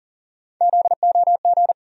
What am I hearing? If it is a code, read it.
Morse code, heard as a keyed tone, about 40 wpm, 700 Hz. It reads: ZOG
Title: USA Call Sign Postfixes - 3 Letters 40wpm